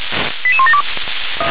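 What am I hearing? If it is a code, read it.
Five-tone selective call (ZVEI), heard as quick runs of five tones, tones of about 70 ms. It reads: 88172
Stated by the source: ZVEI five-tone selective call received by SDR.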